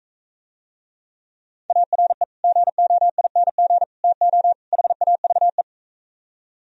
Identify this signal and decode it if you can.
Morse code, heard as a keyed tone, about 44 wpm, 700 Hz. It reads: ARE GOING TO HAVE